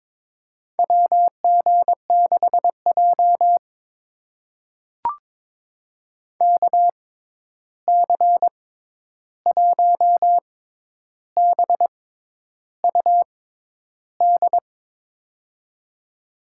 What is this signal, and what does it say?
Morse code, heard as a keyed tone, about 22 wpm, spaced as for 10 wpm, 700 Hz. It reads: WG6J KC1BUD
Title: USA Call Signs - 6x Character Spacing - 22wpm